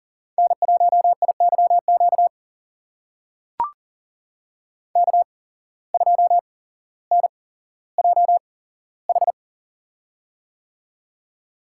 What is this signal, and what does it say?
Morse code, heard as a keyed tone, about 39 wpm, spaced as for 17 wpm, 700 Hz. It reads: N1IYQ K2NJH
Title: USA Call Signs - 5 Characters - 8x Character Spacing - 40wpm